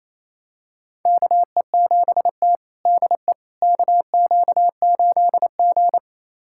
Morse code, heard as a keyed tone, about 28 wpm, 700 Hz. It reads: KE7T DE KQ8G
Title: USA Callsign DE Callsign - 4 Characters - 28wpm